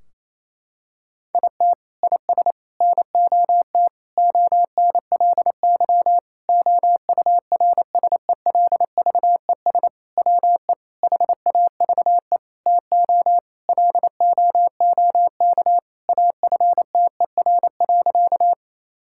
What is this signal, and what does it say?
Morse code, heard as a keyed tone, about 28 wpm, 700 Hz. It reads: IT IS NOT ONLY OURSELVES WE HAVE TO LOOK AFTER.
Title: Sentences from Top 1600 Words 28wpm